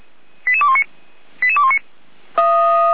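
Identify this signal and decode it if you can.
Five-tone selective call (ZVEI), heard as quick runs of five tones, tones of about 70 ms. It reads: 88219 88219
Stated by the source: ZVEI five-tone call 88140 received on an SDR, Feuerwehr Orlamünde.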